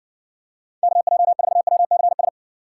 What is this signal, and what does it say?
Morse code, heard as a keyed tone, about 60 wpm, 700 Hz. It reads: KJ4WCS